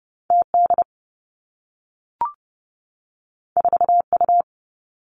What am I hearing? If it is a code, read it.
Morse code, heard as a keyed tone, about 30 wpm, 700 Hz. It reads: T D 4 U
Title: Two Challenging Letter-Number Combinations 30wpm